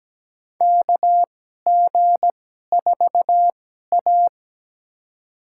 Morse code, heard as a keyed tone, about 17 wpm, spaced as for 12 wpm, 700 Hz. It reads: KG4A
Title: USA Call Signs - 4 Characters - 2x Character Spacing - 17wpm